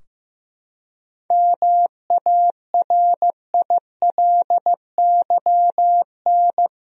Morse code, heard as a keyed tone, about 15 wpm, 700 Hz. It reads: MARILYN
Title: USA Common Names - 15wpm